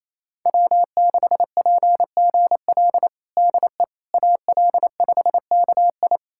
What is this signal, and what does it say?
Morse code, heard as a keyed tone, about 28 wpm, 700 Hz. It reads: W6PGL DE AL5KI
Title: USA Callsign DE Callsign - 5 Characters - 28wpm